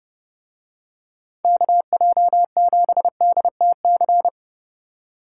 Morse code, heard as a keyed tone, about 30 wpm, 700 Hz. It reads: KJ7DTC